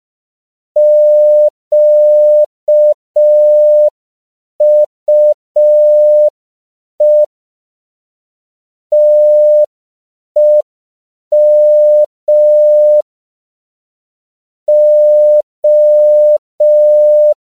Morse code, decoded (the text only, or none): QUE TEM O